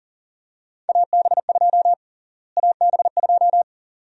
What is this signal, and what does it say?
Morse code, heard as a keyed tone, about 40 wpm, 700 Hz. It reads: AB2 AB2
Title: USA Call Sign Prefixes - Repeated 2x - 40wpm